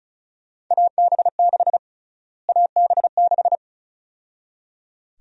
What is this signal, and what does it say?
Morse code, heard as a keyed tone, about 34 wpm, 700 Hz. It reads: AB6 AB6
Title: USA Call Sign Prefixes - Repeated 2x - 35wpm